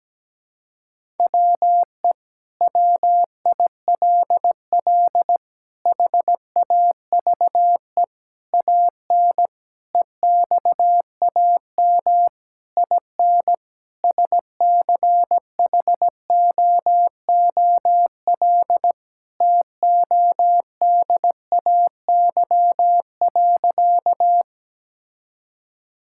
Morse code, heard as a keyed tone, about 17 wpm, 700 Hz. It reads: WE WILL HAVE AN EXAM IN SCHOOL TODAY.
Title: Sentences from Top 1000 Words - Review Part 5 - 17wpm